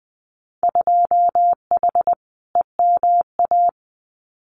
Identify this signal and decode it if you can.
Morse code, heard as a keyed tone, about 20 wpm, 700 Hz. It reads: 2H EMA
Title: Winter Field Day Exchanges 20wpm